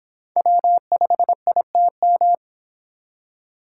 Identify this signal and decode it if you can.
Morse code, heard as a keyed tone, about 25 wpm, 700 Hz. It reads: W5ITM